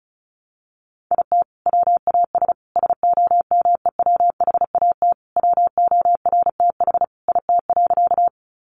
Morse code, heard as a keyed tone, about 34 wpm, 700 Hz. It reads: IT WAS SOMEWHAT WORTH IT.
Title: Sentences from Top 1300 Words 35wpm